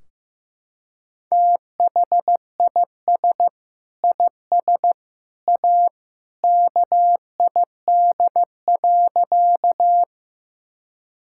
Morse code, heard as a keyed tone, about 15 wpm, 700 Hz. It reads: THIS IS A KID.
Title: Sentences from Top 1600 Words 15wpm